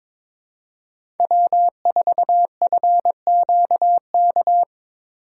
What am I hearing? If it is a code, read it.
Morse code, heard as a keyed tone, about 22 wpm, 700 Hz. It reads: W4FQK
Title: USA Call Signs - 5 Characters - 2x Character Spacing - 22wpm